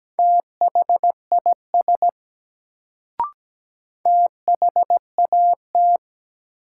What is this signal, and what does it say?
Morse code, heard as a keyed tone, about 17 wpm, 700 Hz. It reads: THIS THAT
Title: Sets of 2 Words - New Word Drill - 17wpm